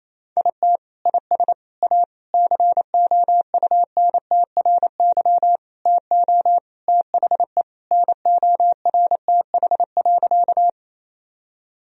Morse code, heard as a keyed tone, about 28 wpm, 700 Hz. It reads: IT IS A COUNTRY TO THE NORTH.
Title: Sentences from Top 1000 Words - Review Part 1 - 28wpm